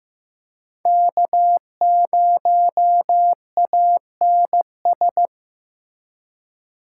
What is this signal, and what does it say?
Morse code, heard as a keyed tone, about 15 wpm, 700 Hz. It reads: K0ANS
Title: USA Call Signs - 5 Characters - 4x Character Spacing - 15wpm